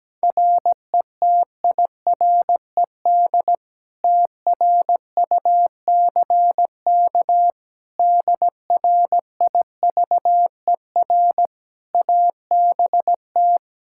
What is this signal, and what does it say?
Morse code, heard as a keyed tone, about 17 wpm, 700 Hz. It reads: RETIRED TRUCK DRIVER ABT
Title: Practice QSO - Exchange 3 - More Info - 17wpm